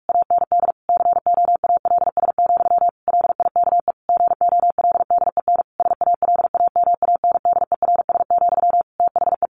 Morse code, heard as a keyed tone, about 44 wpm, 700 Hz. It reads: AND COALS, LIKE GOLDEN SALAMANDERS, THE
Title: A Sound Of Thunder 45wpm